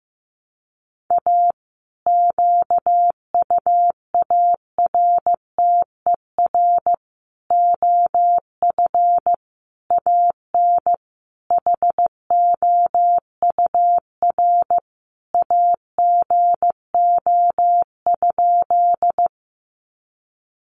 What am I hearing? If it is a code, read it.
Morse code, heard as a keyed tone, about 15 wpm, 700 Hz. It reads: A QUARTER OF AN HOUR AGO?
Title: Sentences from Top 1300 Words 15wpm